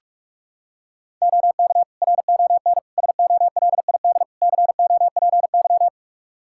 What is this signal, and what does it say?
Morse code, heard as a keyed tone, about 46 wpm, 700 Hz. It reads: OK RON SOLID COPY
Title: Practice QSO - Exchange 3 - More Info - 45wpm